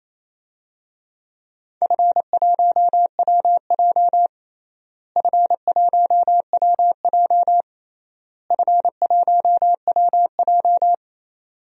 Morse code, heard as a keyed tone, about 27 wpm, 700 Hz. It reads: F1WJ F1WJ F1WJ
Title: DXCC-100 Call Signs - 4 Characters - Repeated 3x - 28wpm